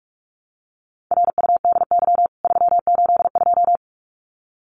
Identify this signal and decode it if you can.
Morse code, heard as a keyed tone, about 44 wpm, 700 Hz. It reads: RUDY 382
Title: CWT Exchange 45wpm